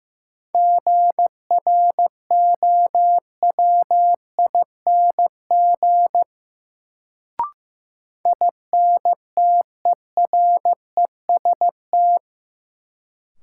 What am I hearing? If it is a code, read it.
Morse code, heard as a keyed tone, about 15 wpm, 700 Hz. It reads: GROWING INTEREST
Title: Sets of 5 Words Encore - New Word Drill - 15wpm